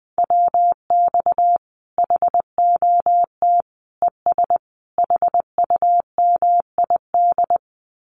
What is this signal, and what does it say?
Morse code, heard as a keyed tone, about 20 wpm, 700 Hz. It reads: WX HOT ES HUMID
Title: Common QSO Phrases 20wpm